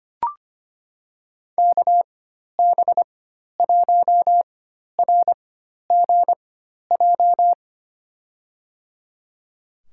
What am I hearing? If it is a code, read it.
Morse code, heard as a keyed tone, about 25 wpm, spaced as for 12 wpm, 700 Hz. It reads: KB1RGJ
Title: USA Call Signs - 6 Characters - 4x Character Spacing - 25wpm